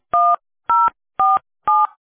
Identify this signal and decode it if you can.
DTMF, heard as telephone touch tones, tones of about 210 ms. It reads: 1047